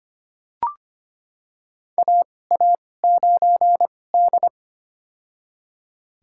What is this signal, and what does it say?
Morse code, heard as a keyed tone, about 25 wpm, spaced as for 18 wpm, 700 Hz. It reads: AA9D
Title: USA Call Signs - 4 Characters - 2x Character Spacing - 25wpm